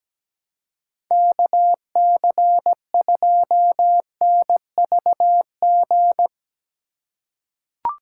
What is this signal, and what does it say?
Morse code, heard as a keyed tone, about 17 wpm, 700 Hz. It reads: KC2NVG
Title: USA Call Signs - 4x Character Spacing - 17wpm